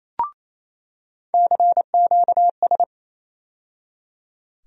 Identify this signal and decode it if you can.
Morse code, heard as a keyed tone, about 28 wpm, 700 Hz. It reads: CQS
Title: USA Call Sign Postfixes - 3 Letters 28wpm